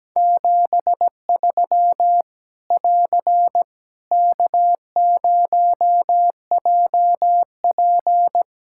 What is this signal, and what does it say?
Morse code, heard as a keyed tone, about 17 wpm, 700 Hz. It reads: 73 <AR> K0JP
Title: Practice QSO - Exchange 4 - Ending - 17wpm